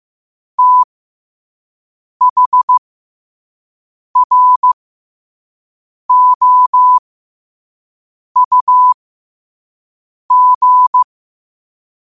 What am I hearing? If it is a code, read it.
Morse code, heard as a keyed tone, about 15 wpm, spaced as for 5 wpm, 1000 Hz. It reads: THROUG
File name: 5wpm